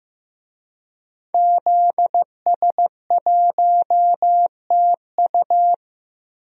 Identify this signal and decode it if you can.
Morse code, heard as a keyed tone, about 15 wpm, 700 Hz. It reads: ZS1TU
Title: DXCC-100 Call Signs - 5 Characters - 15wpm